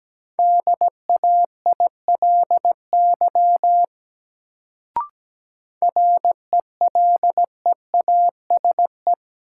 Morse code, heard as a keyed tone, about 17 wpm, 700 Hz. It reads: DAILY RELEASE
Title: Sentences from Top 1100 Words - New Word Drill - 17wpm